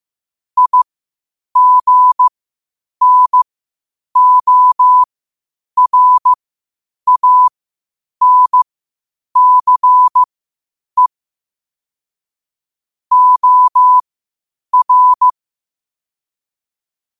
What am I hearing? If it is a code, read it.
Morse code, heard as a keyed tone, about 15 wpm, spaced as for 8 wpm, 1000 Hz. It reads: IGNORANCE OR
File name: qotdfiles / 8wpm